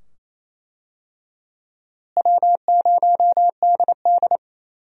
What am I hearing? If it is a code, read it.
Morse code, heard as a keyed tone, about 28 wpm, 700 Hz. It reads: W0DD
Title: USA Call Signs - 4 Characters - Repeated 2x - 28wpm